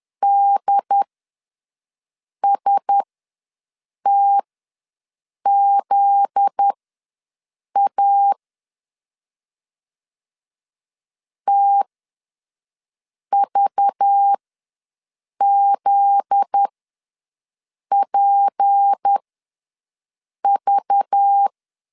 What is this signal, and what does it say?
Morse code, heard as a keyed tone, about 10 wpm, spaced as for 5 wpm, 800 Hz. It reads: DSTZA TVZPV
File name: LEZIONI / 7 GF9Z / 7 GF9Z ws 10wpm 3919